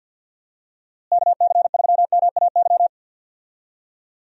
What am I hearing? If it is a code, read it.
Morse code, heard as a keyed tone, about 51 wpm, 700 Hz. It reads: KC3MAY
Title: USA Call Signs - 6 Characters - Repeated 3x - 50wpm